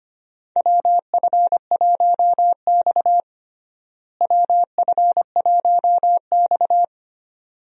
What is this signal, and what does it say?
Morse code, heard as a keyed tone, about 25 wpm, 700 Hz. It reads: WF1X WF1X
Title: USA Call Signs - 4 Characters - Repeated 2x - 25wpm